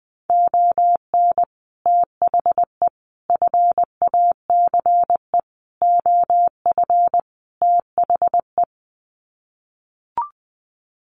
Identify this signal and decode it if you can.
Morse code, heard as a keyed tone, about 20 wpm, 700 Hz. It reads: ON THE FACE OF THE
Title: Sets of 5 Words Encore2 20wpm